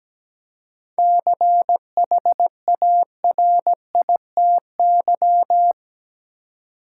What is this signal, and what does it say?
Morse code, heard as a keyed tone, about 17 wpm, 700 Hz. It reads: CHARITY
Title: Seven Letter Words - 17wpm